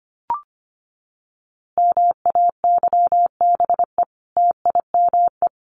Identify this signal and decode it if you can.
Morse code, heard as a keyed tone, about 25 wpm, 700 Hz. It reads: MAYBE TIME
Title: A Sound Of Thunder 25wpm